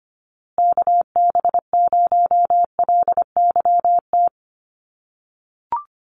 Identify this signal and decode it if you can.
Morse code, heard as a keyed tone, about 25 wpm, 700 Hz. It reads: KB0LYT
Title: USA Call Signs Repeated 3x 25wpm